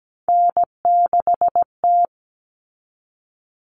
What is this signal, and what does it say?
Morse code, heard as a keyed tone, about 17 wpm, 700 Hz. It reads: N6T
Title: All Cut Numbers - 3-Digit 17wpm